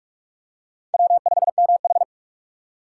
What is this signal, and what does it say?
Morse code, heard as a keyed tone, about 46 wpm, 700 Hz. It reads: W5MH
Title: USA Call Signs - 4 Characters - 4x Character Spacing - 45wpm